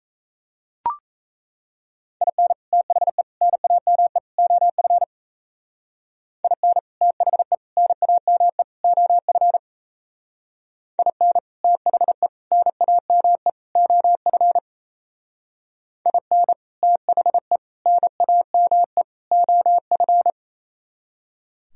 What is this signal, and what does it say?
Morse code, heard as a keyed tone, about 36 wpm, 700 Hz. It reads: IN THE NAME OF IN THE NAME OF IN THE NAME OF IN THE NAME OF